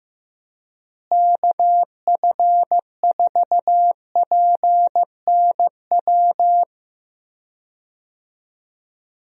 Morse code, heard as a keyed tone, about 15 wpm, 700 Hz. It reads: KF4PNW
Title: USA Call Signs - 6 Characters - Repeated 2x - 15wpm